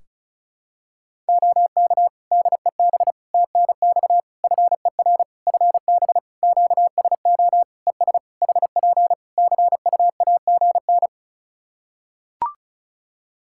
Morse code, heard as a keyed tone, about 36 wpm, 700 Hz. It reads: OK DEB TNX FER FB QSO ES HP CUAGN
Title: Practice QSO - Exchange 4 - Ending - 35wpm